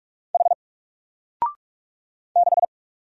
Morse code, heard as a keyed tone, about 44 wpm, 700 Hz. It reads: H 6
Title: Single Letter-Number 45wpm